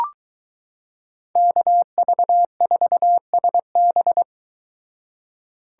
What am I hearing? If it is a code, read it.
Morse code, heard as a keyed tone, about 22 wpm, 700 Hz. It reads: KV4SB